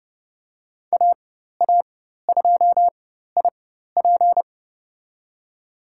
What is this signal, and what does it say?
Morse code, heard as a keyed tone, about 30 wpm, spaced as for 14 wpm, 700 Hz. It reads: AA2IP
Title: USA Call Signs - 5 Characters - 4x Character Spacing - 30wpm